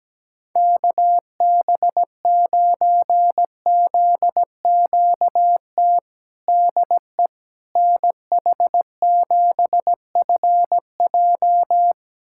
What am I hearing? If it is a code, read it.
Morse code, heard as a keyed tone, about 17 wpm, 700 Hz. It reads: KB9ZQT DE NH7FJ
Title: USA Callsign DE Callsign 17wpm